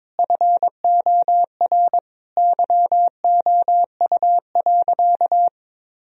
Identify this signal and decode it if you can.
Morse code, heard as a keyed tone, about 22 wpm, 700 Hz. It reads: FOR YOU.
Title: Sentences from Top 1000 Words - Review Part 1 - 22wpm